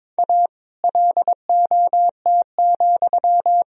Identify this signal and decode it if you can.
Morse code, heard as a keyed tone, about 22 wpm, 700 Hz. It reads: A LOT,
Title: Sentences from Top 1000 Words - Review Part 3 - 22wpm